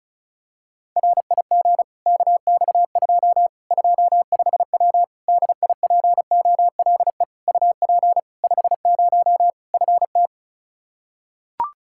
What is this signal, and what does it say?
Morse code, heard as a keyed tone, about 34 wpm, 700 Hz. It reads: RIG KX2 25W DIPOLE UP 50 FT
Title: K1USN SST QSO Practice 35wpm